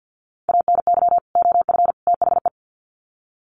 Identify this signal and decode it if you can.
Morse code, heard as a keyed tone, about 50 wpm, 700 Hz. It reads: ANY OF THE
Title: Sets of 3 Words 50wpm